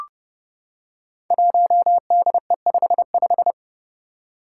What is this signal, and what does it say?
Morse code, heard as a keyed tone, about 28 wpm, 700 Hz. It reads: 1DE55
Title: Five Challenging Letter-Number Combinations 30wpm